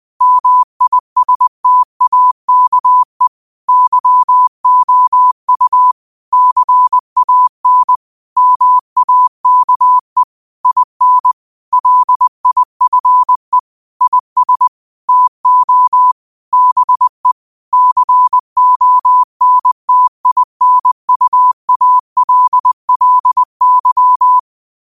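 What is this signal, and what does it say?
Morse code, heard as a keyed tone, about 20 wpm, 1000 Hz. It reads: MISTAKE YOU CAN MAKE IN LIFE IS TO BE CONTINUALLY
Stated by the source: Morse code practice at 20 wpm: famous quotations sent in telegraphy, from URVAG.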